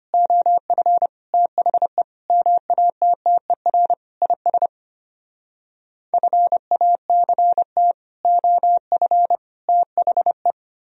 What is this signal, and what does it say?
Morse code, heard as a keyed tone, about 33 wpm, 700 Hz. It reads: OF THE MATTER IS FACT OF THE